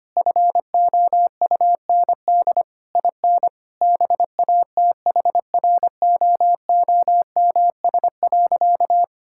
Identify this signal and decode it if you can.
Morse code, heard as a keyed tone, about 25 wpm, 690 Hz. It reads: FOUND IN BATHROOMS.
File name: practice / 25wpm / Difficult Words - Speed-Racing with Hints - 25wpm